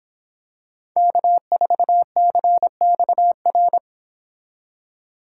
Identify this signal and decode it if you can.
Morse code, heard as a keyed tone, about 26 wpm, 700 Hz. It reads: K4CXR